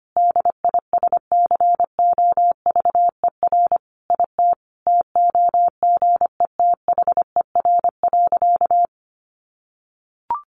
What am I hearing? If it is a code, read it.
Morse code, heard as a keyed tone, about 25 wpm, 700 Hz. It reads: DISCOVER IT TOGETHER.
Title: Sentences from Top 1100 Words 25wpm